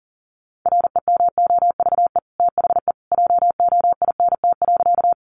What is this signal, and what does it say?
Morse code, heard as a keyed tone, about 40 wpm, 700 Hz. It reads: REMOVE THE JOINT.
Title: Sentences from Top 1300 Words 40wpm